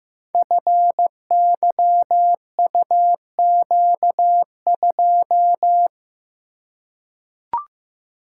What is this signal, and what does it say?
Morse code, heard as a keyed tone, about 15 wpm, 700 Hz. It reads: FYUQ2
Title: Code Groups - 5 Letter:Numbers - 15wpm